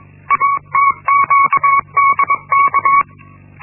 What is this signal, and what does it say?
Morse code, heard as a keyed tone, about 21 wpm, 1110 Hz. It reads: ATQDK
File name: Linksmakalnis-1